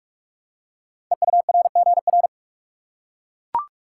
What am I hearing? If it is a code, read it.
Morse code, heard as a keyed tone, about 44 wpm, 690 Hz. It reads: EURGR
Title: Code Groups - 5 Letters - 45wpm